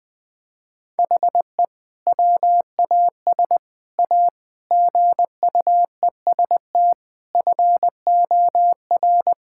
Morse code, heard as a keyed tone, about 21 wpm, 700 Hz. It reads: HE WAS A GUEST FOR
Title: Sentences from Top 1600 Words 20wpm